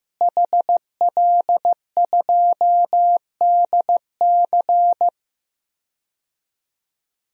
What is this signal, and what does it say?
Morse code, heard as a keyed tone, about 16 wpm, 700 Hz. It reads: HL2DC
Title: DXCC-100 Call Signs - 5 Characters - Repeated 2x - 15wpm